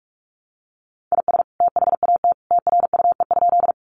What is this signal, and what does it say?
Morse code, heard as a keyed tone, about 44 wpm, 700 Hz. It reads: IS THAT TRUE?
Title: Sentences from Top 400 Words 45wpm